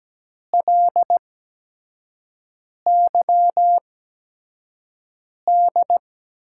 Morse code, heard as a keyed tone, about 17 wpm, spaced as for 5 wpm, 700 Hz. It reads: LYD
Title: USA Call Signs - 6 Characters - 8x Character Spacing - 17wpm